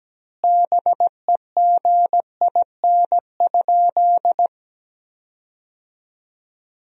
Morse code, heard as a keyed tone, about 17 wpm, 700 Hz. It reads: BEGIN?
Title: 2x Word-Spacing - Sentences from Top 300 Words - 17wpm